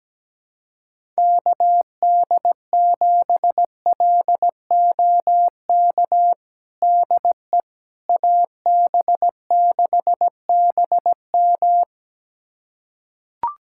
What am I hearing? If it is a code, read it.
Morse code, heard as a keyed tone, about 17 wpm, 700 Hz. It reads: KD7LOK DE AB6BM
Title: USA Callsign DE Callsign 17wpm